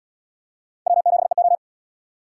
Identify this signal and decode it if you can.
Morse code, heard as a keyed tone, about 76 wpm, 700 Hz. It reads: W7EO